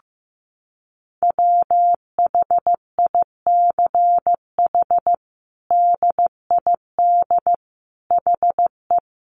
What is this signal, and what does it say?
Morse code, heard as a keyed tone, about 15 wpm, 700 Hz. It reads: WHICH DID HE
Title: Sentences from Top 1100 Words 15wpm